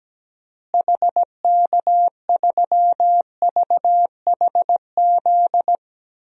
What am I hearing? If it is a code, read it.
Morse code, heard as a keyed tone, about 17 wpm, 700 Hz. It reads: HK3VHZ
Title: DXCC-100 Call Signs - 6 Characters - Repeated 2x - 17wpm